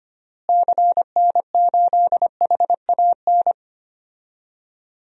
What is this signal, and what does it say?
Morse code, heard as a keyed tone, about 25 wpm, 700 Hz. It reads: CN8HAN
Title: DXCC-100 Call Signs - 6 Characters - Repeated 2x - 25wpm